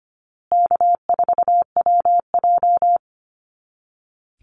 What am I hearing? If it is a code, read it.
Morse code, heard as a keyed tone, about 25 wpm, 700 Hz. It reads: K4WJ
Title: Call Signs Repeated 2x - 25wpm